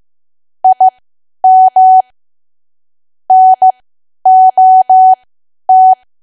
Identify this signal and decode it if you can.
Morse code, heard as a keyed tone, about 15 wpm, spaced as for 10 wpm, 750 Hz. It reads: IM NOT
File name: Archive / 10 WPM / 10WPM